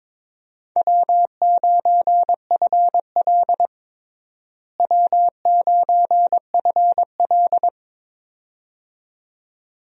Morse code, heard as a keyed tone, about 21 wpm, 700 Hz. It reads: W9FL W9FL
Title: USA Call Signs - 4 Characters - Repeated 2x - 22wpm